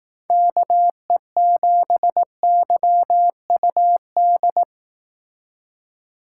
Morse code, heard as a keyed tone, about 18 wpm, 700 Hz. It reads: KE7YUD
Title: Speed Racing - USA 2x3 Call Signs 15wpm